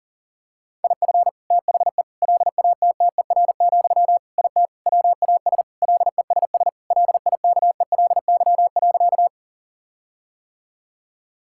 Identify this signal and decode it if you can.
Morse code, heard as a keyed tone, about 40 wpm, 700 Hz. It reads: IF THE LATTER, IT WAS LESS LIKELY.
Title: Sentences from Top 1600 Words 40wpm